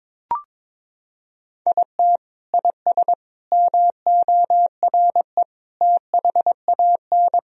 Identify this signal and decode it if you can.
Morse code, heard as a keyed tone, about 22 wpm, 700 Hz. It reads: IT IS MORE THAN
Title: Sentences from Top 1000 Words - Review Part 4 - 22wpm